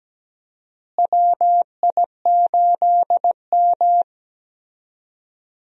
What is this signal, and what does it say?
Morse code, heard as a keyed tone, about 17 wpm, 700 Hz. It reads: WI8M
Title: USA Call Signs - 4 Characters - Repeated 2x - 17wpm